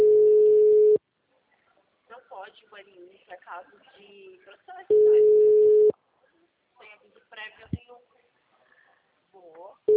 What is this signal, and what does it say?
Ringback tone (Europe, 425 Hz), with speech.